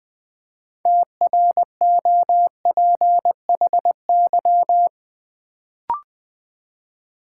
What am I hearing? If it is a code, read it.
Morse code, heard as a keyed tone, about 20 wpm, 700 Hz. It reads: TROPHY
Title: Build A Word - 3 Letters At A Time - 6-Letter Words - 20wpm